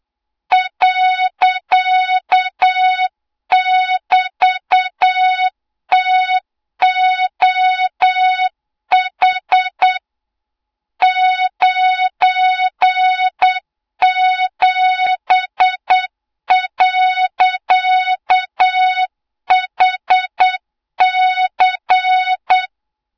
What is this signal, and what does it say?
Morse code, heard as a keyed tone, about 8 wpm, 3700 Hz, with noise. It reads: .<BT>TOH 97.HC
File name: Morze-HA2MN / hang / 20 GYAKORLAT